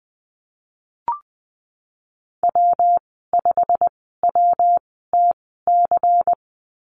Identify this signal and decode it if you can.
Morse code, heard as a keyed tone, about 20 wpm, spaced as for 15 wpm, 700 Hz. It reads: W5WTC
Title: 173 Lesson 28g - Number 9 - Call Signs 20wpm